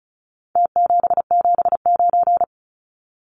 Morse code, heard as a keyed tone, about 34 wpm, 700 Hz. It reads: T779
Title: Numbers - Contest Style - 4-Digit 35wpm